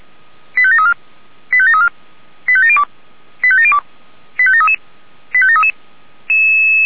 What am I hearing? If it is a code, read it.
Five-tone selective call (ZVEI), heard as quick runs of five tones, tones of about 70 ms. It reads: 85734 85734 85702 85702 85733 85733
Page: SDR/ZVEI